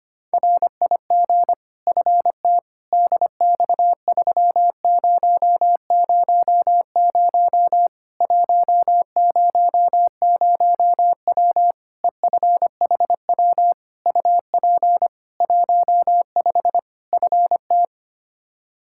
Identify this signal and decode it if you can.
Morse code, heard as a keyed tone, about 25 wpm, 700 Hz. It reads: RIG FT DX3000 100W EFHW UP 15 FT
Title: Practice QSO - Exchange 2 - Info - 25wpm